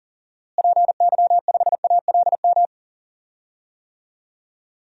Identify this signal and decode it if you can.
Morse code, heard as a keyed tone, about 40 wpm, 700 Hz. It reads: PY5ALM
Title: DXCC-100 Call Signs - 6 Characters - Repeated 3x - 40wpm